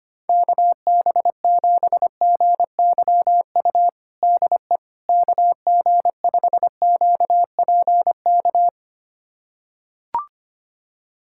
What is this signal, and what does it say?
Morse code, heard as a keyed tone, about 25 wpm, 700 Hz. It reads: KB7GYU DE KG5QPK
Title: USA Callsign DE Callsign - 6 Characters - 25wpm